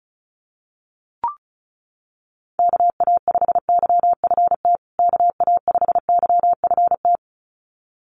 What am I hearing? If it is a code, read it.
Morse code, heard as a keyed tone, about 34 wpm, 700 Hz. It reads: KA5YFT KA5YFT
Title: USA Call Signs Repeated 2x 35wpm